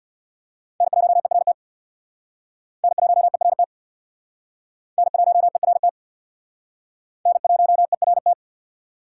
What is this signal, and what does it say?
Morse code, heard as a keyed tone, about 59 wpm, 700 Hz. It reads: N1ERT N1ERT N1ERT N1ERT
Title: Speed Racing - USA Call Signs 50wpm